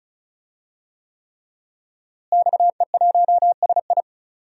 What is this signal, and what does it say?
Morse code, heard as a keyed tone, about 36 wpm, 700 Hz. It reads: XE1SI
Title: DXCC-100 Call Signs - 5 Characters - Repeated 3x - 35wpm